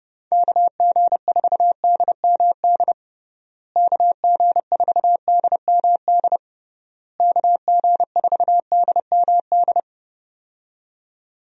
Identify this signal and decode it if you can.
Morse code, heard as a keyed tone, about 30 wpm, 700 Hz. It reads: KG4DMD KG4DMD KG4DMD
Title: USA Call Signs - 6 Characters - Repeated 3x - 30wpm